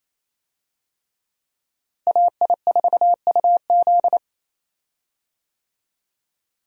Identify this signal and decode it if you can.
Morse code, heard as a keyed tone, about 27 wpm, 700 Hz. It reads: AI4UZ
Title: USA Call Signs - 5 Characters - Repeated 3x - 28wpm